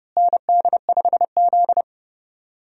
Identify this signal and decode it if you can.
Morse code, heard as a keyed tone, about 30 wpm, 700 Hz. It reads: ND5Z
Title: USA Call Signs - 4 Characters - Repeated 2x - 30wpm